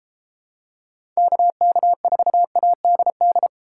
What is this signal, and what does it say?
Morse code, heard as a keyed tone, about 31 wpm, 700 Hz. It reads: KK4ADD